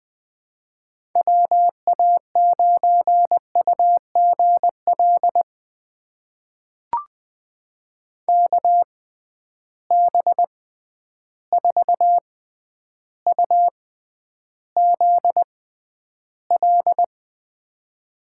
Morse code, heard as a keyed tone, about 21 wpm, spaced as for 10 wpm, 700 Hz. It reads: WA9UGL KB4UZL
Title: USA Call Signs - 6x Character Spacing - 20wpm